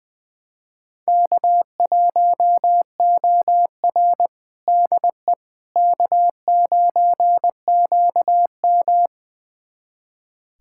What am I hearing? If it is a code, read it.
Morse code, heard as a keyed tone, about 20 wpm, 700 Hz. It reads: K1OR DE K9QM
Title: USA Callsign DE Callsign - 4 Characters - 20wpm